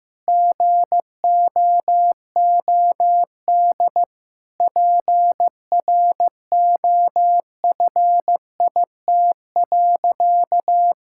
Morse code, heard as a keyed tone, about 15 wpm, 700 Hz. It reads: GOOD PROFIT.
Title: Sentences from Top 1600 Words 15wpm